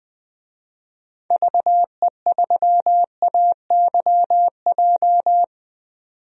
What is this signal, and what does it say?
Morse code, heard as a keyed tone, about 20 wpm, 700 Hz. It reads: VE3AYJ
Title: DXCC-100 Call Signs - 6 Characters - 20wpm